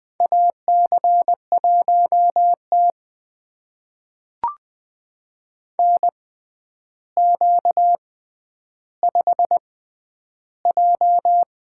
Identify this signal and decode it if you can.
Morse code, heard as a keyed tone, about 20 wpm, spaced as for 10 wpm, 700 Hz. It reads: AC1T NQ5J
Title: USA Call Signs - 4 Characters - 6x Character Spacing - 20wpm